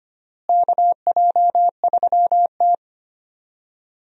Morse code, heard as a keyed tone, about 25 wpm, 700 Hz. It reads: KJ3T
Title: USA Call Signs - 4 Characters - Repeated 3x - 25wpm